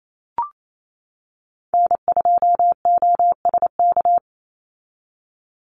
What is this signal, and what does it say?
Morse code, heard as a keyed tone, about 28 wpm, 700 Hz. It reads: N2OSK
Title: USA Call Signs 28wpm